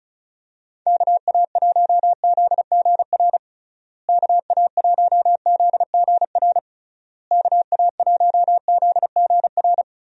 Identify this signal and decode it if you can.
Morse code, heard as a keyed tone, about 34 wpm, 700 Hz. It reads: KA1ZGR KA1ZGR KA1ZGR
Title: USA Call Signs - 6 Characters - Repeated 3x - 35wpm